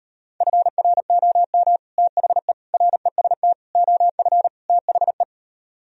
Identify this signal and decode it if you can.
Morse code, heard as a keyed tone, about 32 wpm, 700 Hz. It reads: FROM THE REST OF THE